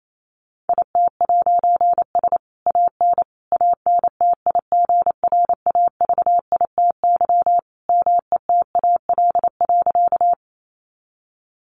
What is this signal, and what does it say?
Morse code, heard as a keyed tone, about 28 wpm, 700 Hz. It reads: IT'S AN ANTIGRAVITY METAL.
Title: A Sound Of Thunder 28wpm